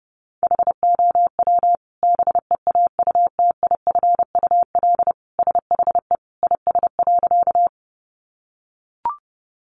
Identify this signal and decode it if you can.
Morse code, heard as a keyed tone, about 30 wpm, 700 Hz. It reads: HOW BEAUTIFUL SHE IS.
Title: Sentences from Top 900 Words 30wpm